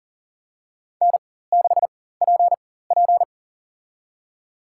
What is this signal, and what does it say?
Morse code, heard as a keyed tone, about 40 wpm, spaced as for 19 wpm, 700 Hz. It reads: N6PP
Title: USA Call Signs - 4 Characters - 4x Character Spacing - 40wpm